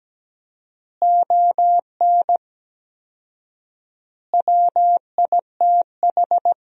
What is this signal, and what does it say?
Morse code, heard as a keyed tone, about 17 wpm, 700 Hz. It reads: ON WITH
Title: 4x Word-Spacing - Sentences from Top 300 Words - 17wpm